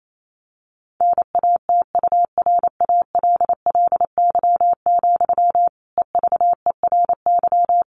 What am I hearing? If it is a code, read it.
Morse code, heard as a keyed tone, about 28 wpm, 700 Hz. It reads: NATURALLY, EVERY
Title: Sentences from Top 1500 Words 28wpm